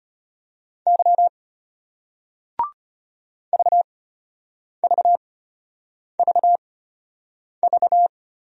Morse code, heard as a keyed tone, about 34 wpm, spaced as for 8 wpm, 700 Hz. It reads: Y VVVV